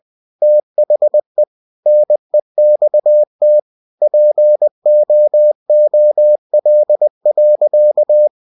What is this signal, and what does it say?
Morse code, heard as a keyed tone, about 20 wpm, 600 Hz. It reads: THE NEXT POOL.